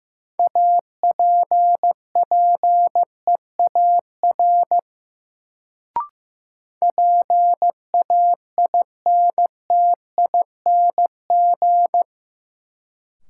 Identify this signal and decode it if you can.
Morse code, heard as a keyed tone, about 16 wpm, 700 Hz. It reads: APPEAR PAINTING
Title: Sentences from Top 1200 Words - New Word Drill Alt - 15wpm